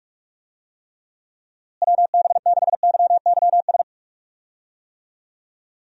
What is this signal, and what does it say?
Morse code, heard as a keyed tone, about 46 wpm, 700 Hz. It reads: WB6YYS
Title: USA Call Signs - 6 Characters - Repeated 2x - 45wpm